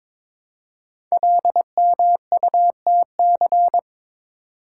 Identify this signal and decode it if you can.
Morse code, heard as a keyed tone, about 22 wpm, 700 Hz. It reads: LMUTC
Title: Code Groups - 5 Letters - 22wpm